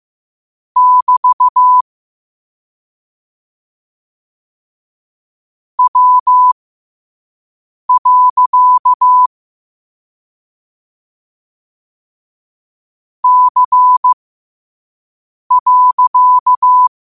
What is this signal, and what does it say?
Morse code, heard as a keyed tone, about 15 wpm, spaced as for 5 wpm, 1000 Hz. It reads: <BT> W. C.
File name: qotdfiles / 5wpm